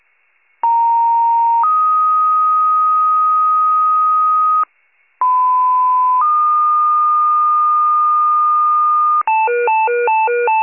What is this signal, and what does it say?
Two-tone page, heard as a one-second tone, then a three-second tone, about 930 then 1250 Hz, about 1010 then 1220 Hz. It is heard from 0.6 to 9.2 s.